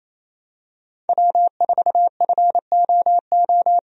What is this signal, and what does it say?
Morse code, heard as a keyed tone, about 28 wpm, 700 Hz. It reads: W4FOO
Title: USA Call Signs - 5 Characters - Repeated 3x - 28wpm